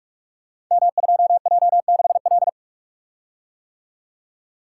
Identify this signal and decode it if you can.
Morse code, heard as a keyed tone, about 45 wpm, 700 Hz. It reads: M2J6L
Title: Code Groups - 5 Letter:Numbers - 45wpm